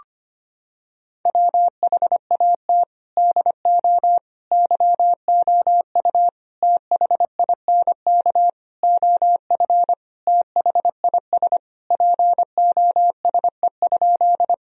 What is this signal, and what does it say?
Morse code, heard as a keyed tone, about 25 wpm, 700 Hz. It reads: WHAT DO YOU THINK OF THIS POSE?
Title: Sentences from Top 1000 Words - Review Part 2 - 25wpm